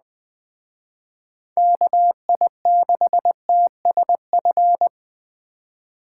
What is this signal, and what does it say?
Morse code, heard as a keyed tone, about 20 wpm, 700 Hz. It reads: KI6TSF
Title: Speed Racing - USA 2x3 Call Signs 17wpm